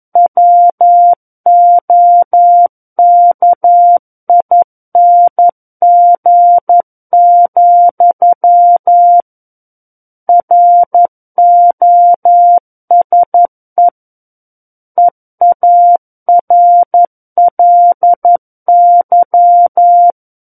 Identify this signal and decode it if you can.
Morse code, heard as a keyed tone, about 11 wpm, 700 Hz. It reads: WOKING, ROSE EARLY
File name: 11 WPM